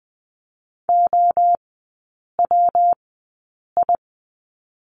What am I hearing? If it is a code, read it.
Morse code, heard as a keyed tone, about 20 wpm, spaced as for 8 wpm, 700 Hz. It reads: OWI
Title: 218 Lesson 35e - Number 0 - Three Characters 20wpm